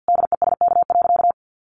Morse code, heard as a keyed tone, about 50 wpm, 700 Hz. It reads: DESK.
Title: A Sound Of Thunder 50wpm